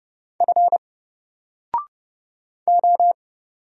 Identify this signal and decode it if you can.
Morse code, heard as a keyed tone, about 31 wpm, 700 Hz. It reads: F O
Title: Single Character (Letter:Number:Punctuation:Prosign) 30wpm